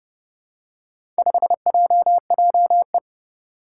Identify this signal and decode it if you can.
Morse code, heard as a keyed tone, about 30 wpm, 690 Hz. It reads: 5JJE
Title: Four Challenging Letter-Number Combinations 30wpm